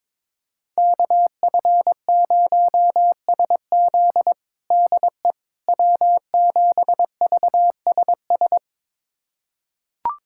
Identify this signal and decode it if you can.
Morse code, heard as a keyed tone, about 22 wpm, 700 Hz. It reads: KF0SZ DE W7VSS
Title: USA Callsign DE Callsign - 5 Characters - 22wpm